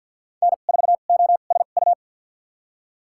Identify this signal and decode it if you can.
Morse code, heard as a keyed tone, about 53 wpm, spaced as for 37 wpm, 700 Hz. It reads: N4OSU
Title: USA Call Signs - 5 Characters - 2x Character Spacing - 50wpm